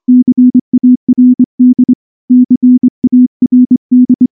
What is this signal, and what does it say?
Morse code, heard as a keyed tone, about 24 wpm, 250 Hz. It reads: CARD CARD
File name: morse-training-25WPM